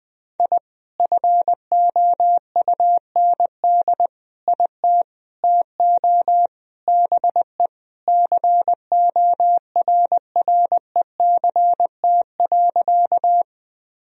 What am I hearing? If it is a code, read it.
Morse code, heard as a keyed tone, about 20 wpm, 700 Hz. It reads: I FOUND IT TO BE CORRECT.
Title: Sentences from Top 1000 Words - Review Part 2 - 20wpm